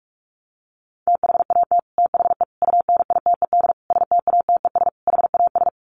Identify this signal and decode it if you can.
Morse code, heard as a keyed tone, about 44 wpm, 700 Hz. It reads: THAT THE UNITED STATES HAS
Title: Sets of 5 Words Encore 45wpm